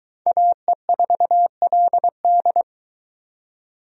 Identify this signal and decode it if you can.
Morse code, heard as a keyed tone, about 22 wpm, 700 Hz. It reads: AE4LD